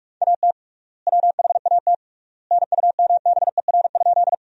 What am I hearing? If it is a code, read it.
Morse code, heard as a keyed tone, about 44 wpm, 700 Hz. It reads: AT WHAT NUMBER?
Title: 3x Word-Spacing - Sentences from Top 100 Words - 45wpm